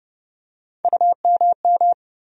Morse code, heard as a keyed tone, about 30 wpm, 700 Hz. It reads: UMM
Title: USA Call Sign Postfixes - 3 Letters 30wpm